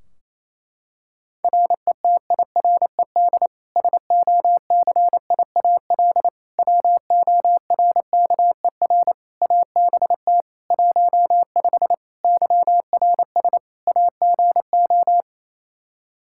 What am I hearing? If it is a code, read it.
Morse code, heard as a keyed tone, about 28 wpm, 700 Hz. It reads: RETIRED SOCIAL WORKER ABT 15 YRS AGO
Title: Practice QSO - Exchange 3 - More Info - 28wpm